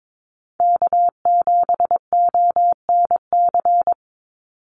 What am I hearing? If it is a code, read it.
Morse code, heard as a keyed tone, about 22 wpm, 700 Hz. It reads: K7ONC
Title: USA Call Signs 22wpm